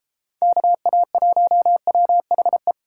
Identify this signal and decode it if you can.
Morse code, heard as a keyed tone, about 33 wpm, 700 Hz. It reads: KA1WHE